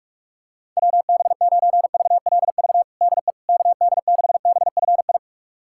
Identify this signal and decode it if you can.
Morse code, heard as a keyed tone, about 46 wpm, 700 Hz. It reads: WB9VLV DE KD6BFI
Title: USA Callsign DE Callsign 45wpm